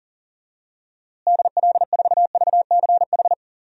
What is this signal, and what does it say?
Morse code, heard as a keyed tone, about 40 wpm, 700 Hz. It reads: DL4VCH
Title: DXCC-100 Call Signs - 6 Characters - Repeated 2x - 40wpm